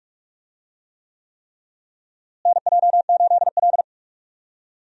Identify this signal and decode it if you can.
Morse code, heard as a keyed tone, about 47 wpm, 700 Hz. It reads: NJ8L
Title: USA Call Signs - 4 Characters - Repeated 3x - 45wpm